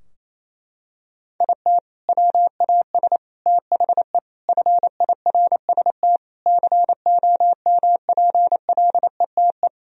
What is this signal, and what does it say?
Morse code, heard as a keyed tone, about 28 wpm, 700 Hz. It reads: IT WAS THE FIRST COMPLETE
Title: Sentences from Top 1000 Words - Review Part 2 - 28wpm